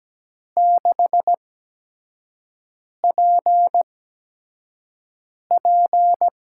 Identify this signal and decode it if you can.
Morse code, heard as a keyed tone, about 17 wpm, 700 Hz. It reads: 6 P P
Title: USA Call Signs - 8x Character Spacing - 17wpm